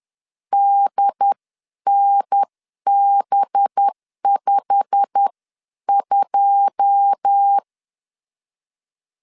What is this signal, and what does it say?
Morse code, heard as a keyed tone, about 10 wpm, spaced as for 9 wpm, 800 Hz. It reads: DNB52